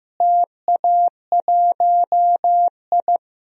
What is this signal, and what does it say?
Morse code, heard as a keyed tone, about 15 wpm, 700 Hz. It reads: TA1I
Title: DXCC-100 Call Signs - 4 Characters - Repeated 3x - 15wpm